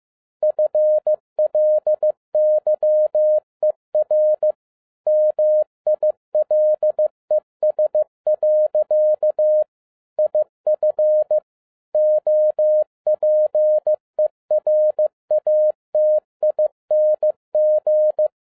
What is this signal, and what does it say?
Morse code, heard as a keyed tone, about 15 wpm, 610 Hz. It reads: FLYER MILES. IF OPERATING
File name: 2011-07-mp3 / n12 T8XO 15wpm